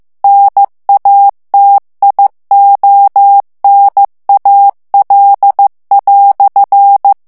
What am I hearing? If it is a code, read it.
Morse code, heard as a keyed tone, about 15 wpm, 800 Hz. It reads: NATIONAL"
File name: morse mp3 / FRANCAIS TEXTE CLAIR 1 VITESSE 14 MOTS MINUTES